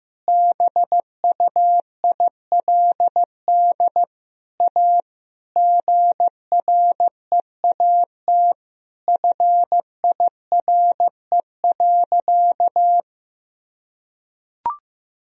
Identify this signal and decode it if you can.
Morse code, heard as a keyed tone, about 16 wpm, 700 Hz. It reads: BUILD A GREAT FIRE.
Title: Sentences from Top 1000 Words - Review Part 1 - 15wpm